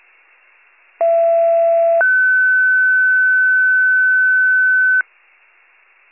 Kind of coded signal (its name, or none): two-tone page